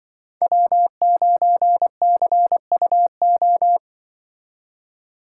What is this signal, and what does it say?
Morse code, heard as a keyed tone, about 24 wpm, 700 Hz. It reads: W9CUO
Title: Speed Racing - USA 1x3 Call Signs 20wpm